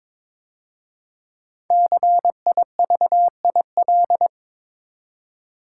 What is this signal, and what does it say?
Morse code, heard as a keyed tone, about 22 wpm, 690 Hz. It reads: CIVIL
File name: 5 Letter Words - First 3 Letters Given - 22wpm